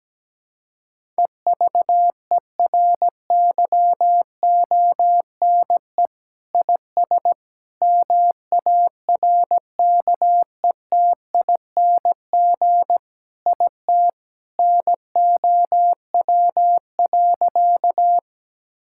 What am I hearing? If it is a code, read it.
Morse code, heard as a keyed tone, about 17 wpm, 700 Hz. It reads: EVERYONE IS MARKETING IT NOW.
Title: Sentences from Top 1000 Words - Review Part 5 - 17wpm